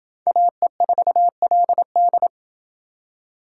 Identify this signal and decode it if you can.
Morse code, heard as a keyed tone, about 28 wpm, 700 Hz. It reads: AE4LD